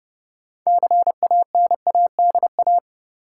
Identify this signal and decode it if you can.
Morse code, heard as a keyed tone, about 30 wpm, 700 Hz. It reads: CANADA
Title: Sets of 5 Words - New Word Drill - 30wpm